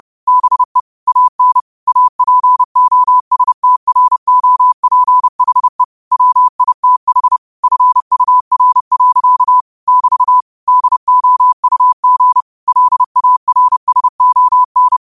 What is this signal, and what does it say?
Morse code, heard as a keyed tone, about 30 wpm, 1000 Hz. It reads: BE AN APOSTROPHE WITH FUR. <BT> DOUG LARSON
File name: qotdfiles / 30wpm